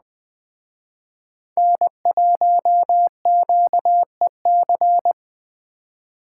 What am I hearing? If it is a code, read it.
Morse code, heard as a keyed tone, about 21 wpm, 700 Hz. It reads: N1QEC